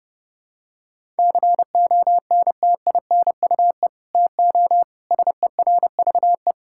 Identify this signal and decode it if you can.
Morse code, heard as a keyed tone, about 32 wpm, 700 Hz. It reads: CONTINUE TO SERVE
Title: Sentences from Top 1000 Words - Review Part 6 - 30wpm